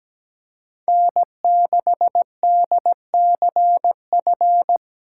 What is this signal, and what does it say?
Morse code, heard as a keyed tone, about 17 wpm, 700 Hz. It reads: N6DCF
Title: USA Call Signs - 5 Characters - 2x Character Spacing - 17wpm